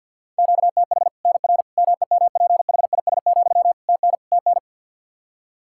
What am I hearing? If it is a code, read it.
Morse code, heard as a keyed tone, about 53 wpm, 700 Hz. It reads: QTH NR MEMPHIS, TN TN
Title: Practice QSO - Exchange 1 - Beginning - 50wpm